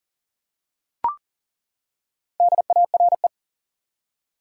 Morse code, heard as a keyed tone, about 40 wpm, 700 Hz. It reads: DARE
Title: Four Letter Words - 40wpm